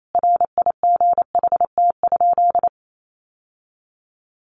Morse code, heard as a keyed tone, about 28 wpm, 700 Hz. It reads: RIGHT?
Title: Sentences from Top 800 Words 28wpm